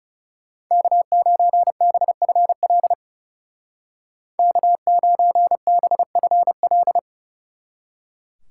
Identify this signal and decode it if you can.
Morse code, heard as a keyed tone, about 40 wpm, 700 Hz. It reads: K9BFL K9BFL